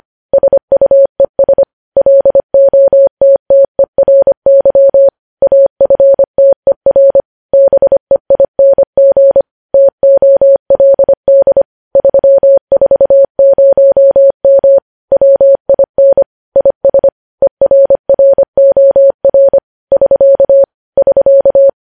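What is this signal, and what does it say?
Morse code, heard as a keyed tone, about 25 wpm, 550 Hz. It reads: SUES LOTTERY AFTER BEING TOLD 340M WIN IS ERROR <SK> <SK>
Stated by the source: Morse code news headlines sent at 25, opening with vvv.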